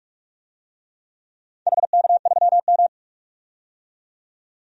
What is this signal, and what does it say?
Morse code, heard as a keyed tone, about 43 wpm, 700 Hz. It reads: HK3M
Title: DXCC-100 Call Signs - 4 Characters - 45wpm